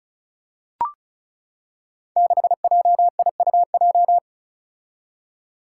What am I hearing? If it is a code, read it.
Morse code, heard as a keyed tone, about 36 wpm, 700 Hz. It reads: 6JIUJ
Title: Five Challenging Letter-Number Combinations 35wpm